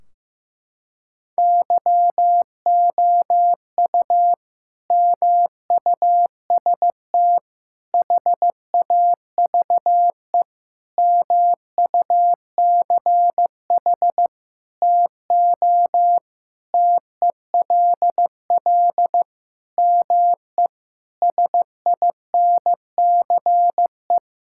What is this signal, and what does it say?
Morse code, heard as a keyed tone, about 15 wpm, 700 Hz. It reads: YOU MUST HAVE MUCH TO TELL ME SINCE
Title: Sentences from Top 1600 Words 15wpm